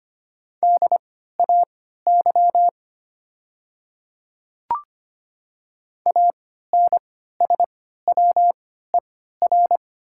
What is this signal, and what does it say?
Morse code, heard as a keyed tone, about 25 wpm, spaced as for 14 wpm, 700 Hz. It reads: DAY ANSWER
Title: 3x Character Spacing - Top 300 Words - 25wpm